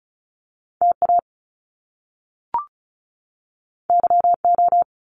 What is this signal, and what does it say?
Morse code, heard as a keyed tone, about 36 wpm, 700 Hz. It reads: TA YO
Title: Top Two Letter Combinations in Words 35wpm